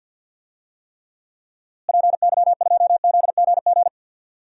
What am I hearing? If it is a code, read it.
Morse code, heard as a keyed tone, about 47 wpm, 700 Hz. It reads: PY2ZGG
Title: DXCC-100 Call Signs - 6 Characters - Repeated 2x - 50wpm